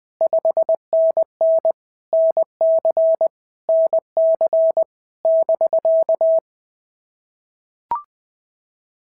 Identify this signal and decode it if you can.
Morse code, heard as a keyed tone, about 20 wpm, 650 Hz. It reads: 5NN NC NC <BK>
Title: POTA QSO Practice - 20wpm